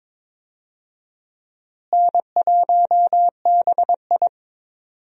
Morse code, heard as a keyed tone, about 22 wpm, 700 Hz. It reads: N1BI
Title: USA Call Signs - 4 Characters - Repeated 2x - 22wpm